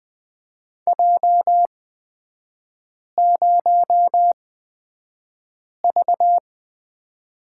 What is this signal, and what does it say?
Morse code, heard as a keyed tone, about 20 wpm, spaced as for 5 wpm, 700 Hz. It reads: J0V
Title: 224 Lesson 36e - BT - Three Characters 20wpm